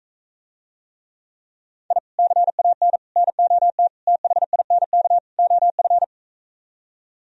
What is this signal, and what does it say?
Morse code, heard as a keyed tone, about 45 wpm, 700 Hz. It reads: I CAN NOT THINK OF